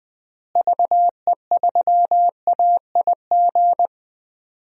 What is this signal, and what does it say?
Morse code, heard as a keyed tone, about 21 wpm, 700 Hz. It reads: VE3AIG
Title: DXCC-100 Call Signs - 6 Characters - Repeated 3x - 20wpm